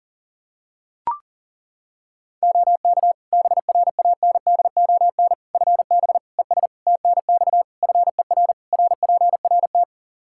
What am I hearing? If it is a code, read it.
Morse code, heard as a keyed tone, about 40 wpm, 700 Hz. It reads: OK BRANDON FB ES TNX FER RPRT
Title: Practice QSO - Exchange 2 - Info - 40wpm